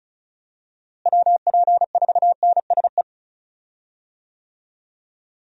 Morse code, heard as a keyed tone, about 34 wpm, 700 Hz. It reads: WP4NSE
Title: USA Call Signs - 6 Characters - Repeated 2x - 35wpm